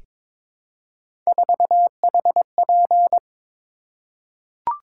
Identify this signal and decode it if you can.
Morse code, heard as a keyed tone, about 22 wpm, 700 Hz. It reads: 4HP
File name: practice / 22wpm / Three Letter-Number Combinations - 22wpm